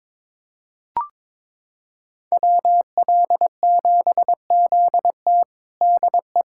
Morse code, heard as a keyed tone, about 21 wpm, 700 Hz. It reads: WL7ZT DE
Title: USA Callsign DE Callsign - 5 Characters - 22wpm